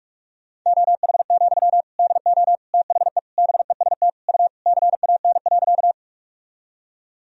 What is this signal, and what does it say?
Morse code, heard as a keyed tone, about 41 wpm, 700 Hz. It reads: OH, DO THE BEST U CAN.
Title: Sentences from Top 500 Words and Abbreviations 45wpm